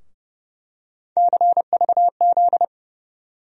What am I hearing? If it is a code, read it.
Morse code, heard as a keyed tone, about 32 wpm, 700 Hz. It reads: CVZ
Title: USA Call Sign Postfixes - 3 Letters 30wpm